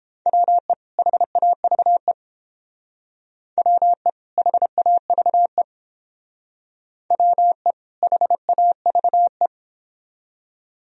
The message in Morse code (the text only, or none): WE HAVE WE HAVE WE HAVE